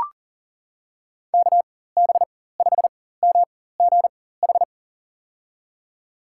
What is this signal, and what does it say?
Morse code, heard as a keyed tone, about 40 wpm, spaced as for 19 wpm, 690 Hz. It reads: KB5MGH
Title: USA Call Signs - 4x Character Spacing - 40wpm